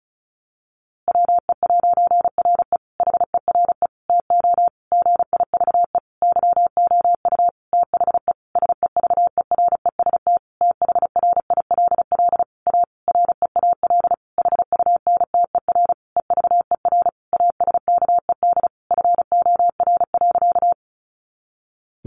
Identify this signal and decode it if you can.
Morse code, heard as a keyed tone, about 34 wpm, 700 Hz. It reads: WE'RE HERE TO GIVE YOU THE SEVEREST THRILL A REAL HUNTER EVER ASKED FOR.
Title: A Sound Of Thunder 35wpm